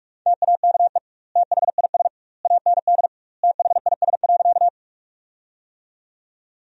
Morse code, heard as a keyed tone, about 43 wpm, 700 Hz. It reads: TAKE THIS AND THIS.
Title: 2x Word-Spacing - Sentences from Top 200 Words - 45wpm